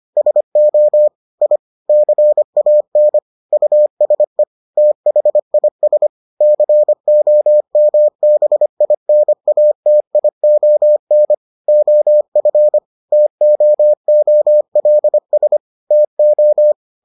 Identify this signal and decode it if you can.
Morse code, heard as a keyed tone, about 25 wpm, 600 Hz. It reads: SO I CAN USE THIS COMBINATION OF TOOLS TO